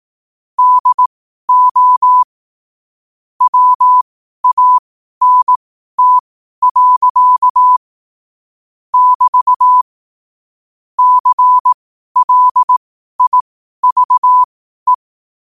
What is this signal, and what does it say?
Morse code, heard as a keyed tone, about 18 wpm, spaced as for 12 wpm, 1000 Hz. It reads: DO WANT. <BT> CLIVE
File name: qotdfiles / 12wpm